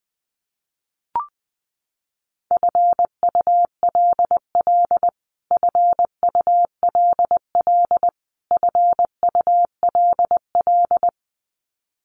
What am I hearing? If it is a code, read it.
Morse code, heard as a keyed tone, about 20 wpm, 700 Hz. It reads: FULL FULL FULL
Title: Top 500 Words Repeated 20wpm